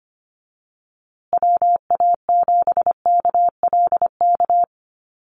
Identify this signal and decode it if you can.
Morse code, heard as a keyed tone, about 25 wpm, 700 Hz. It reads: WA7KLK
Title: USA Call Signs Repeated 3x 25wpm